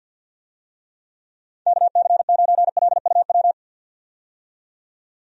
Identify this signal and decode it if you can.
Morse code, heard as a keyed tone, about 48 wpm, 700 Hz. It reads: KC9LUW
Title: USA Call Signs - 6 Characters - 50wpm